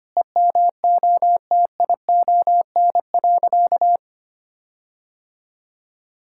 Morse code, heard as a keyed tone, about 25 wpm, 700 Hz. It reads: EMOTION.
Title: Sentences from Top 1600 Words 25wpm